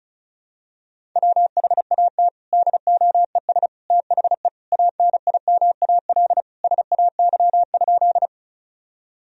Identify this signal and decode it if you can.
Morse code, heard as a keyed tone, about 34 wpm, 700 Hz. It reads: WHAT DOES THE ANIMAL SAY?
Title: Sentences from Top 1000 Words - Review Part 1 - 35wpm